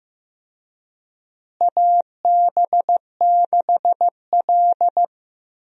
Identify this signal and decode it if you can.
Morse code, heard as a keyed tone, about 15 wpm, 700 Hz. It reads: AB6L
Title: USA Call Signs - 4 Characters - Repeated 3x - 15wpm